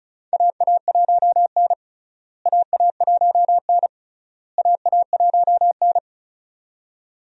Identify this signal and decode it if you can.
Morse code, heard as a keyed tone, about 34 wpm, 700 Hz. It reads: AA1N AA1N AA1N
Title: USA Call Signs - 4 Characters - Repeated 3x - 35wpm